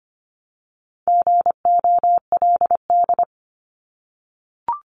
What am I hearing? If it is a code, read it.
Morse code, heard as a keyed tone, about 25 wpm, 700 Hz. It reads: GOLD
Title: Multi-letter Phonemes Repeated 3x 25wpm